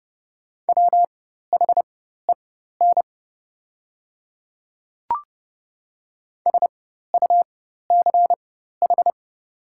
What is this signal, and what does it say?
Morse code, heard as a keyed tone, about 30 wpm, spaced as for 14 wpm, 690 Hz. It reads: WHEN SUCH
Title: 4x Character Spacing - Top 200 Words - 30wpm